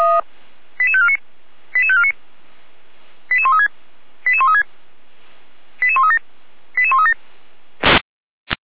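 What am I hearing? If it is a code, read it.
Five-tone selective call (ZVEI), heard as quick runs of five tones, tones of about 70 ms. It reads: 88549 88549 88136 88136 88137 88137
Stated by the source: ZVEI five-tone selective call received via SDR.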